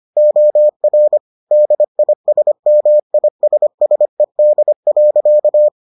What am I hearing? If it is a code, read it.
Morse code, heard as a keyed tone, about 25 wpm, 600 Hz. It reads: OR DISMISSED.